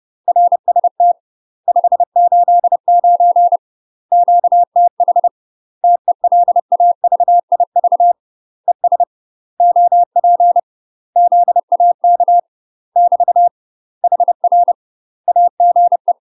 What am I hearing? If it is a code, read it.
Morse code, heard as a keyed tone, about 30 wpm, 700 Hz. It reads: RST 589 QTH TELAVIV ES OP ZAK <BT> HR AGE
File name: UK MASTER QSOs TRAINING/30WPM